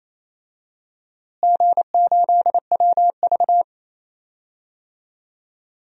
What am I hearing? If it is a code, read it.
Morse code, heard as a keyed tone, about 28 wpm, 700 Hz. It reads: G8WV
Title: DXCC-100 Call Signs - 4 Characters - Repeated 3x - 28wpm